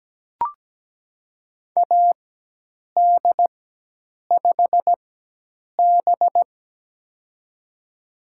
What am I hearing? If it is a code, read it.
Morse code, heard as a keyed tone, about 17 wpm, spaced as for 8 wpm, 700 Hz. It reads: AD5B
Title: USA Call Signs - 4 Characters - 4x Character Spacing - 17wpm